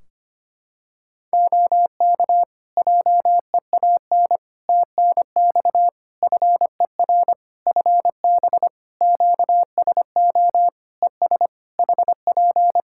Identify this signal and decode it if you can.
Morse code, heard as a keyed tone, about 25 wpm, 700 Hz. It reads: OK JEAN TNX FER FB QSO ES HP
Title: Practice QSO - Exchange 4 - Ending - 25wpm